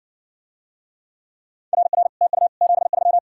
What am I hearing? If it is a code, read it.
Morse code, heard as a keyed tone, about 60 wpm, 700 Hz. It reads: RR TU 73